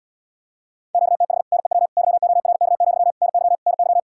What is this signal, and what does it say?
Morse code, heard as a keyed tone, about 80 wpm, 700 Hz. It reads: QTH NEW YORK, NY NY